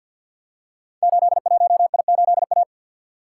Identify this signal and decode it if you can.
Morse code, heard as a keyed tone, about 51 wpm, 700 Hz. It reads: 81I8A
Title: Code Groups - 5 Letter:Numbers - 50wpm